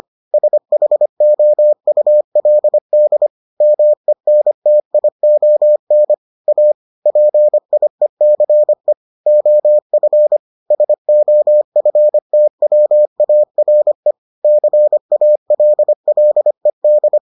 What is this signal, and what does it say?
Morse code, heard as a keyed tone, about 25 wpm, 600 Hz. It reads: SHOULD MENTION A PIECE OF SOFTWARE CALLED